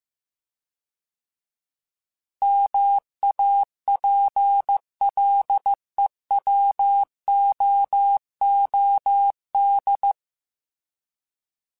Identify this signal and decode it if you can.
Morse code, heard as a keyed tone, about 15 wpm, 800 Hz. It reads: MAPLEWOOD